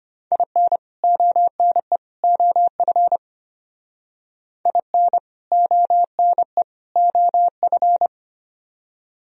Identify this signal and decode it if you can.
Morse code, heard as a keyed tone, about 32 wpm, 700 Hz. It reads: IN ONE OF IN ONE OF